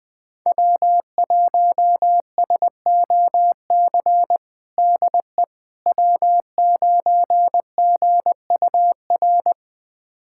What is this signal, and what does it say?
Morse code, heard as a keyed tone, about 20 wpm, 700 Hz. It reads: W1SOC DE W9GUR
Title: USA Callsign DE Callsign - 5 Characters - 20wpm